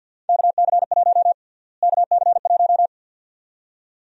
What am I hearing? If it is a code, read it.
Morse code, heard as a keyed tone, about 53 wpm, 700 Hz. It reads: KC1 KC1
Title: USA Call Sign Prefixes - Repeated 2x - 50wpm